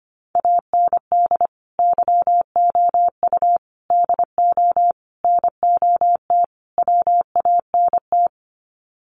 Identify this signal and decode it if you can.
Morse code, heard as a keyed tone, about 25 wpm, 700 Hz. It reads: AND YOU DO NOT WANT
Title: Sets of 5 Words Encore2 25wpm